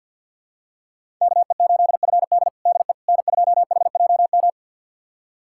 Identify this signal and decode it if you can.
Morse code, heard as a keyed tone, about 52 wpm, 690 Hz. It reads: KE8FD DE N2HJM
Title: USA Callsign DE Callsign - 5 Characters - 50wpm